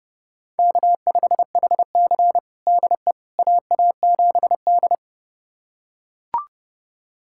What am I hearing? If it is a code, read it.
Morse code, heard as a keyed tone, about 30 wpm, 700 Hz. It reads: K5HC DE AA7D
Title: USA Callsign DE Callsign - 4 Characters - 30wpm